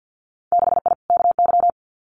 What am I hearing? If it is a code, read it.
Morse code, heard as a keyed tone, about 50 wpm, 700 Hz. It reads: 6I KY
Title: Winter Field Day Exchanges 50wpm